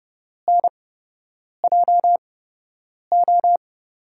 Morse code, heard as a keyed tone, about 30 wpm, spaced as for 8 wpm, 700 Hz. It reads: NJO
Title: USA Call Signs - 5 Characters - 8x Character Spacing - 30wpm